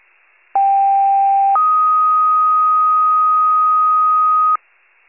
Two-tone page, heard as a one-second tone, then a three-second tone, about 790 then 1250 Hz. It is heard from 0.5 to 4.6 s.